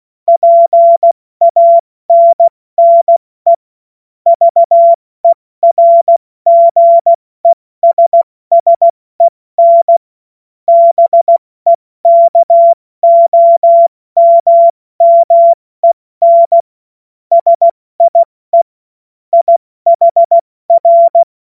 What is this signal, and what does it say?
Morse code, heard as a keyed tone, about 16 wpm, 680 Hz. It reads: PANNE VERGESSEN BEKOMMEN SIE IHR
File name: Morsekurs / Witze / Witz 05 Dezember 2023 70 F80